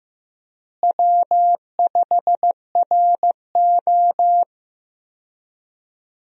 Morse code, heard as a keyed tone, about 15 wpm, 700 Hz. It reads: W5RO
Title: USA Call Signs - 4 Characters - Repeated 2x - 15wpm